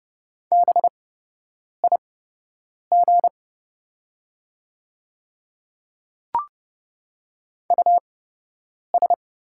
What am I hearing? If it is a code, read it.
Morse code, heard as a keyed tone, about 30 wpm, spaced as for 8 wpm, 700 Hz. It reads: BIG US
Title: 8x Character Spacing - Top 200 Words - 30wpm